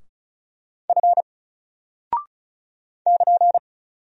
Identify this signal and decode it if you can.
Morse code, heard as a keyed tone, about 36 wpm, 700 Hz. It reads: F <KN>
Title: Single Character (Letter:Number:Punctuation:Prosign) Alt 35wpm